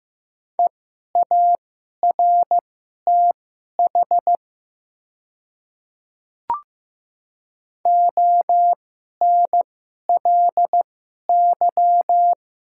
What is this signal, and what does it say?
Morse code, heard as a keyed tone, about 15 wpm, spaced as for 11 wpm, 700 Hz. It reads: EARTH ONLY
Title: 2x Character Spacing - Top 300 Words - 15wpm